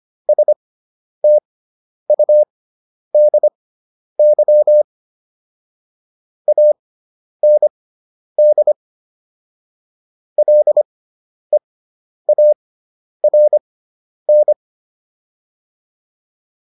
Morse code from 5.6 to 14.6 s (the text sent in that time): AND LEARN